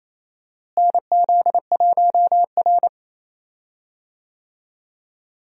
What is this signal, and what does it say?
Morse code, heard as a keyed tone, about 28 wpm, 700 Hz. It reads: NZ1R
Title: USA Call Signs - 4 Characters - Repeated 3x - 28wpm